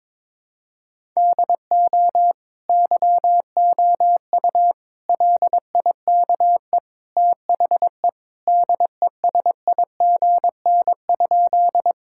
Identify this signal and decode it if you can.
Morse code, heard as a keyed tone, about 22 wpm, 700 Hz. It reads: DO YOU LIKE THE DESIGN?
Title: Sentences from Top 1000 Words - Review Part 5 - 22wpm